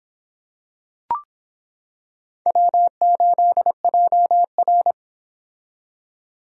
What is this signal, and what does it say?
Morse code, heard as a keyed tone, about 25 wpm, 700 Hz. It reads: W8JR